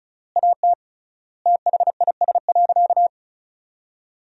Morse code, heard as a keyed tone, about 36 wpm, 700 Hz. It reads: AT THIS.
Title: 3x Word-Spacing - Sentences from Top 300 Words - 35wpm